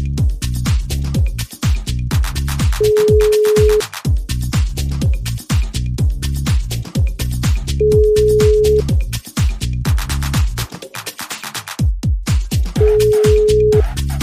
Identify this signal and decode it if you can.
Ringback tone (Europe, 425 Hz).